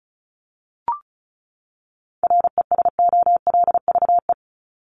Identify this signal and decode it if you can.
Morse code, heard as a keyed tone, about 34 wpm, 700 Hz. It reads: RESOLVE
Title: Top 1751-2000 Words 35wpm